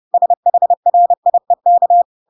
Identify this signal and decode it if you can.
Morse code, heard as a keyed tone, about 30 wpm, 700 Hz. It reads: SHRIEK
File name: Pinocchio / Pinocchio ch5 30 30 ik1hge